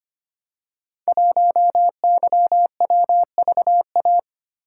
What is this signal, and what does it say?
Morse code, heard as a keyed tone, about 25 wpm, 700 Hz. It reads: 1YWVA
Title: Code Groups - 5 Letter:Numbers - 25wpm